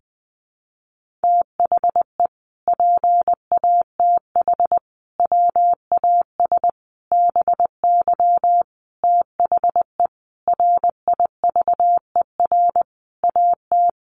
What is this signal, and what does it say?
Morse code, heard as a keyed tone, about 20 wpm, 700 Hz. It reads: THE PATH WAS BY THE RIVER AT
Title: Sentences from Top 1400 Words 20wpm